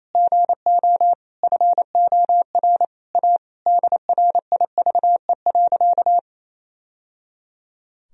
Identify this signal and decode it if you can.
Morse code, heard as a keyed tone, about 28 wpm, 700 Hz. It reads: GO FOR A DRIVE.
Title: Sentences from Top 1000 Words - Review Part 2 - 28wpm